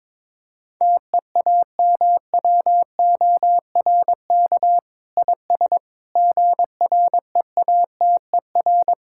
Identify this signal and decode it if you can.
Morse code, heard as a keyed tone, about 21 wpm, 700 Hz. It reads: TEAMWORK IS GREATER
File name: Sentences with 7-Word Phrases - 22wpm